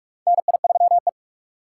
Morse code, heard as a keyed tone, about 45 wpm, 700 Hz. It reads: NI3E
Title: USA Call Signs - 3x Character Spacing - 45wpm